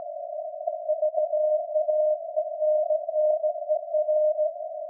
Morse code, heard as a keyed tone, about 17 wpm, 610 Hz. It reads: VAENNER
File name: iz7slz / VLF / SAQ jun 2019